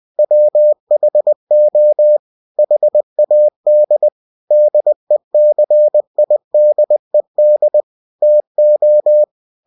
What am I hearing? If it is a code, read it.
Morse code, heard as a keyed tone, about 20 wpm, 600 Hz. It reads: WHO HAD DECIDED TO
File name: cw-books / winnie-the-pooh / winnie-20wpm-page-0071